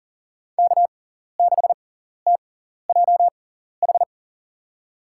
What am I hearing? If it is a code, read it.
Morse code, heard as a keyed tone, about 40 wpm, spaced as for 14 wpm, 700 Hz. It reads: K6TJH
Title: USA Call Signs - 5 Characters - 6x Character Spacing - 40wpm